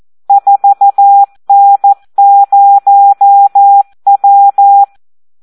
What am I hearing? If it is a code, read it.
Morse code, heard as a keyed tone, about 14 wpm, 800 Hz. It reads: 4N0W
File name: Audio practica CW / SD 1 14